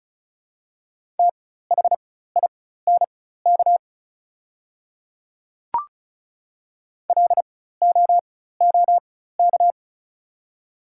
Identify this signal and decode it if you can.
Morse code, heard as a keyed tone, about 36 wpm, spaced as for 16 wpm, 700 Hz. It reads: THINK LOOK
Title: 4x Character Spacing - Top 300 Words - 35wpm